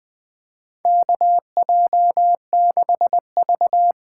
Morse code, heard as a keyed tone, about 20 wpm, 700 Hz. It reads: KJ6V
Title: USA Call Signs - 6 Characters - 8x Character Spacing - 20wpm